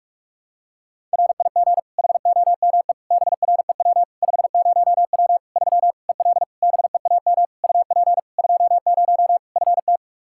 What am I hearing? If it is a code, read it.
Morse code, heard as a keyed tone, about 46 wpm, 700 Hz. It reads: RIG HOME BREW 50W 3 EL BEAM UP 20 FT
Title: Practice QSO - Exchange 2 - Info - 45wpm